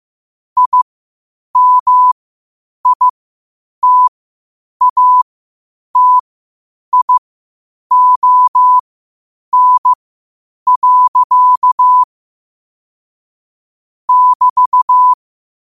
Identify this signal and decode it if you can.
Morse code, heard as a keyed tone, about 15 wpm, spaced as for 8 wpm, 1000 Hz. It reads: IMITATION. <BT>
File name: qotdfiles / 8wpm